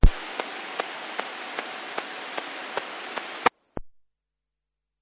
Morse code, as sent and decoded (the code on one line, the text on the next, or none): none